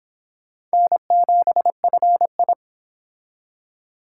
Morse code, heard as a keyed tone, about 25 wpm, 700 Hz. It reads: N7FI